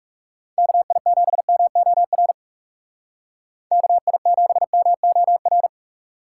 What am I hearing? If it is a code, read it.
Morse code, heard as a keyed tone, about 37 wpm, 700 Hz. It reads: KI7MOR KI7MOR
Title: Speed Racing - USA 2x3 Call Signs 30wpm